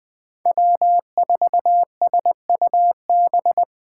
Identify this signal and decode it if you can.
Morse code, heard as a keyed tone, about 20 wpm, 700 Hz. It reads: W4SUB
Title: USA Call Signs - 5 Characters - 3x Character Spacing - 20wpm